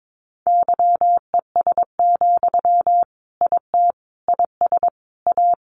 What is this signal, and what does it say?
Morse code, heard as a keyed tone, about 22 wpm, 700 Hz. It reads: YES, IT IS A
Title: Sentences from Top 1400 Words 22wpm